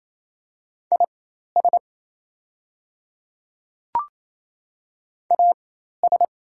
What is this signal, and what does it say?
Morse code, heard as a keyed tone, about 28 wpm, spaced as for 13 wpm, 700 Hz. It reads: IS AS
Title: 4x Character Spacing - Top 200 Words - 28wpm